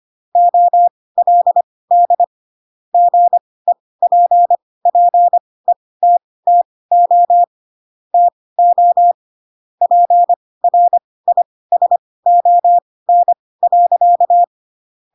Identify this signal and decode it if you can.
Morse code, heard as a keyed tone, about 25 wpm, spaced as for 18 wpm, 700 Hz. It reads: OLD GEPPETTO TO PRISON.
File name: Pinocchio / Pinocchio ch4 12 25 ik1hge